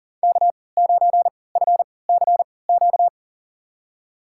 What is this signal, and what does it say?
Morse code, heard as a keyed tone, about 40 wpm, spaced as for 23 wpm, 700 Hz. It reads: K9FCQ
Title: USA Call Signs - 3x Character Spacing - 40wpm